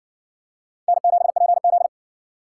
Morse code, heard as a keyed tone, about 60 wpm, 700 Hz. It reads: N7PZ